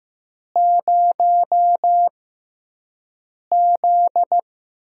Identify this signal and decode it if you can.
Morse code, heard as a keyed tone, about 15 wpm, spaced as for 5 wpm, 700 Hz. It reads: 0Z
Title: USA Call Signs - 4 Characters - 6x Character Spacing - 15wpm